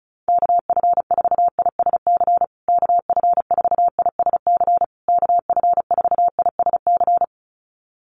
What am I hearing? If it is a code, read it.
Morse code, heard as a keyed tone, about 34 wpm, 700 Hz. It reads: KF4ISC KF4ISC KF4ISC
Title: USA Call Signs Repeated 3x 35wpm